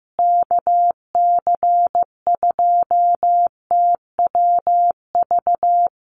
Morse code, heard as a keyed tone, about 15 wpm, 700 Hz. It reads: KC2TWV
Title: USA Call Signs Repeated 3x 15wpm